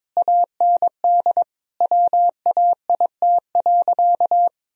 Morse code, heard as a keyed tone, about 21 wpm, 700 Hz. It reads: AND WAIT.
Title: Sentences from Top 1000 Words - Review Part 4 - 22wpm